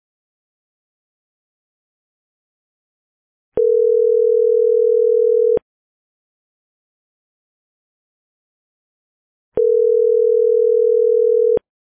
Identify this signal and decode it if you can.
Ringback tone (North America), 440 + 480 Hz.